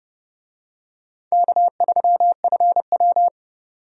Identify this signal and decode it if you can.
Morse code, heard as a keyed tone, about 31 wpm, 700 Hz. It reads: K3FW
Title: USA Call Signs - 4 Characters - 30wpm